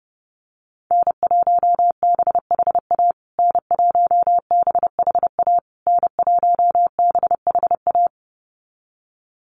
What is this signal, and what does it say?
Morse code, heard as a keyed tone, about 30 wpm, 700 Hz. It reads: N1BHA N1BHA N1BHA
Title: USA Call Signs Repeated 3x 30wpm